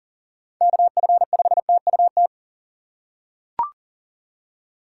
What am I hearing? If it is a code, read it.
Morse code, heard as a keyed tone, about 40 wpm, 700 Hz. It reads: KF5TUT
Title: USA Call Signs - 6 Characters - 4x Character Spacing - 40wpm